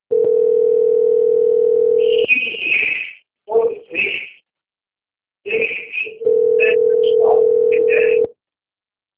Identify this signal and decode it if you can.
Ringback tone (North America), 440 + 480 Hz, with speech.
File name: out- - - - - .46559